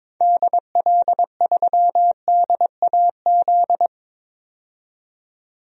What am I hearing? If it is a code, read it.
Morse code, heard as a keyed tone, about 22 wpm, 700 Hz. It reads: DL3DAZ
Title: DXCC-100 Call Signs - 6 Characters - 22wpm